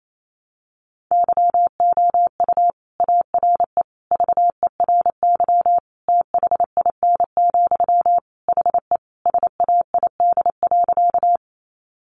Morse code, heard as a keyed tone, about 28 wpm, 700 Hz. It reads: YOU ARE VERY THIN, HE SAID.
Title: Sentences from Top 1200 Words 28wpm